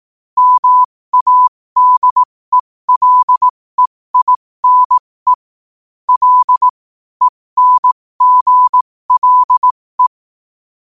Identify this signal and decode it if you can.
Morse code, heard as a keyed tone, about 18 wpm, spaced as for 15 wpm, 1000 Hz. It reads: MADELEINE L ENGLE
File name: qotdfiles / 15wpm